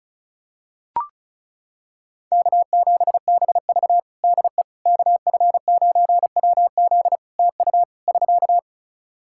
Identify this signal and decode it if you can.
Morse code, heard as a keyed tone, about 34 wpm, 700 Hz. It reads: K7BV DE KF9WZ TU <SK>
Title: K1USN SST QSO Practice 35wpm